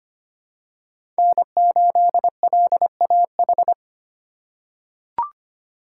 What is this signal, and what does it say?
Morse code, heard as a keyed tone, about 25 wpm, 700 Hz. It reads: N8LAH
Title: USA Call Signs - 5 Characters - 8x Character Spacing - 25wpm